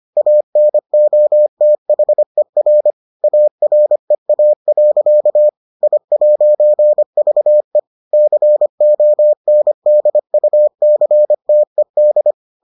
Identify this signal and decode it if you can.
Morse code, heard as a keyed tone, about 25 wpm, 600 Hz. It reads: ANOTHER AREA. I'VE CONDUCTED